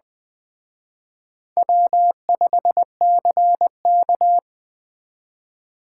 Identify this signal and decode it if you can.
Morse code, heard as a keyed tone, about 20 wpm, 700 Hz. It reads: W5CK